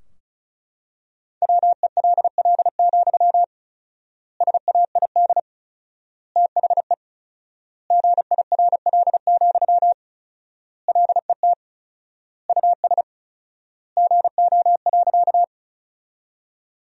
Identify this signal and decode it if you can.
Morse code, heard as a keyed tone, about 34 wpm, 700 Hz. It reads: WELL, SAID THE GIRL, LET US GO.
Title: 4x Word-Spacing - Sentences from Top 400 Words - 35wpm